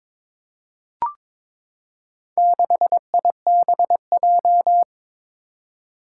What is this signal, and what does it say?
Morse code, heard as a keyed tone, about 22 wpm, 700 Hz. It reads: 6IBJ
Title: Four Challenging Letter-Number Combinations 22wpm